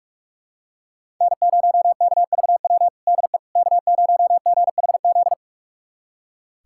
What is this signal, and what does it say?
Morse code, heard as a keyed tone, about 46 wpm, 700 Hz. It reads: N0KVW DE K0GHZ
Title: USA Callsign DE Callsign - 5 Characters - 45wpm